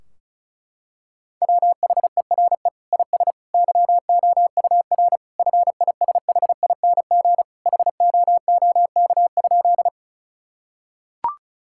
Morse code, heard as a keyed tone, about 36 wpm, 700 Hz. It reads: WHERE IS YOUR FISHING HOOK?
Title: Sentences from Top 1600 Words 35wpm